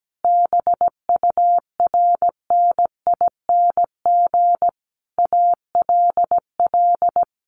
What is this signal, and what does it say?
Morse code, heard as a keyed tone, about 17 wpm, 700 Hz. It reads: BURNING ALL
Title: A Sound Of Thunder 17wpm